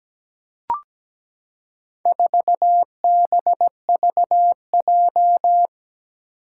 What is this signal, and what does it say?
Morse code, heard as a keyed tone, about 17 wpm, 700 Hz. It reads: 4BVJ
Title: Four Challenging Letter-Number Combinations 17wpm